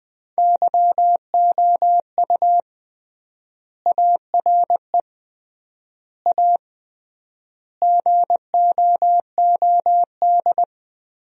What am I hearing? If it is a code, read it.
Morse code, heard as a keyed tone, about 21 wpm, 700 Hz. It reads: YOU ARE A GOOD
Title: 3x Word-Spacing - Sentences from Top 200 Words - 20wpm